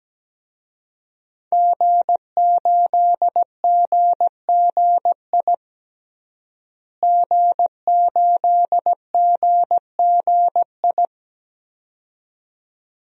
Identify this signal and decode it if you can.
Morse code, heard as a keyed tone, about 17 wpm, 700 Hz. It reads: G8GGI G8GGI
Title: DXCC-100 Call Signs - 5 Characters - Repeated 2x - 17wpm